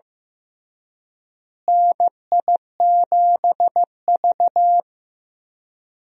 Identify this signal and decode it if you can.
Morse code, heard as a keyed tone, about 15 wpm, 700 Hz. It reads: NI7V
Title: USA Call Signs - 4 Characters - Repeated 3x - 15wpm